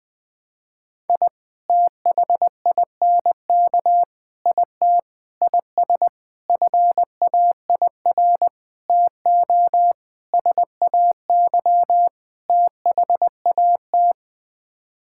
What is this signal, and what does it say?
Morse code, heard as a keyed tone, about 20 wpm, 700 Hz. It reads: I THINK IT IS FAIR TO SAY THAT
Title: Sets of 8 Words - 20wpm